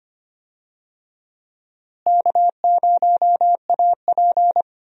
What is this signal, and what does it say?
Morse code, heard as a keyed tone, about 25 wpm, 700 Hz. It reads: K0AP
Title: USA Call Signs - 4 Characters - Repeated 2x - 25wpm